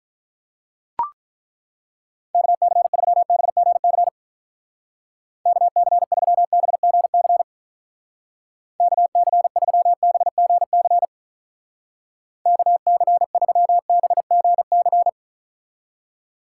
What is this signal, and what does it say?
Morse code, heard as a keyed tone, about 42 wpm, 700 Hz. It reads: KC3BGC KC3BGC KC3BGC KC3BGC
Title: Speed Racing - USA 2x3 Call Signs 35wpm